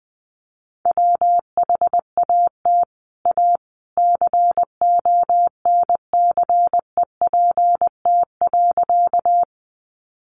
Morse code, heard as a keyed tone, about 20 wpm, 700 Hz. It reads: WHAT A CONCEPT.
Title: Sentences from Top 1100 Words 20wpm